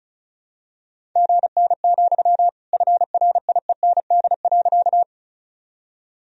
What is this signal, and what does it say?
Morse code, heard as a keyed tone, about 36 wpm, 700 Hz. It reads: GN, FRIEND.
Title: Sentences from Top 300 Words and Abbreviations 35wpm